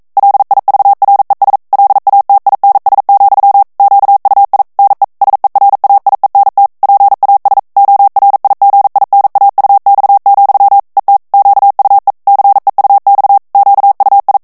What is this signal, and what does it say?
Morse code, heard as a keyed tone, about 42 wpm, 800 Hz. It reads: LIVRES LATINS, QUI NE SERAIENT PAS ORIGINAUX, ET QUE CEUX QUI
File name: CW / FICHIERS-MP3 / FRANCAIS TEXTE CLAIR 2 VITESSE 40 MOTS MINUTES